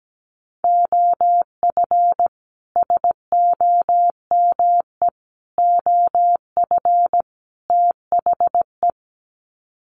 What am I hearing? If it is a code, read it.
Morse code, heard as a keyed tone, about 17 wpm, 700 Hz. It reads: OF SOME OF THE
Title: Sets of 4 Words 17wpm